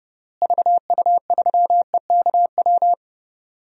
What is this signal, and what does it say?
Morse code, heard as a keyed tone, about 30 wpm, 700 Hz. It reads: VU3EKW
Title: DXCC-100 Call Signs - 6 Characters - Repeated 3x - 30wpm